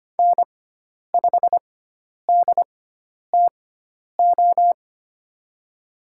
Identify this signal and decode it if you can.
Morse code, heard as a keyed tone, about 25 wpm, spaced as for 10 wpm, 700 Hz. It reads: N5DTO
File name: Lesson 16g - Letter c - Call Signs 25wpm